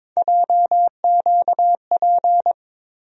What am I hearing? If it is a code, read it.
Morse code, heard as a keyed tone, about 22 wpm, 700 Hz. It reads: JQP
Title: USA Call Sign Postfixes - 3 Letters 22wpm